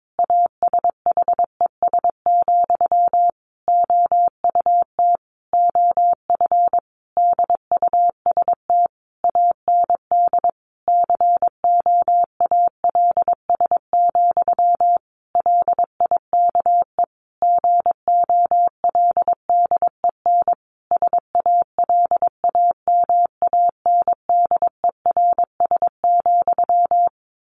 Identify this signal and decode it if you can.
Morse code, heard as a keyed tone, about 22 wpm, 700 Hz. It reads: ASHES, OUT OF DUST AND COALS, LIKE GOLDEN SALAMANDERS,
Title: A Sound Of Thunder 22wpm